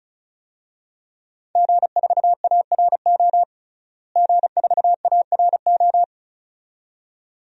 Morse code, heard as a keyed tone, about 36 wpm, 700 Hz. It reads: G4ARO G4ARO
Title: DXCC-100 Call Signs - 5 Characters - Repeated 2x - 35wpm